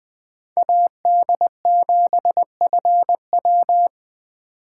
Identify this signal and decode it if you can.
Morse code, heard as a keyed tone, about 20 wpm, 700 Hz. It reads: AD7FW